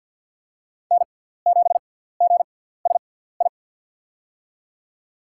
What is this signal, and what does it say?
Morse code, heard as a keyed tone, about 47 wpm, spaced as for 17 wpm, 700 Hz. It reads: N7GSI
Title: USA Call Signs - 5 Characters - 6x Character Spacing - 50wpm